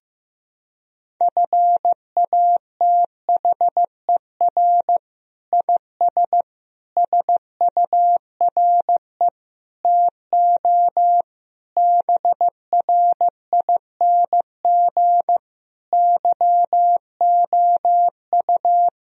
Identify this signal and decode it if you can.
Morse code, heard as a keyed tone, about 16 wpm, 700 Hz. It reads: FATHER IS SURE TO BRING YOU
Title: Sentences from Top 1000 Words - Review Part 3 - 15wpm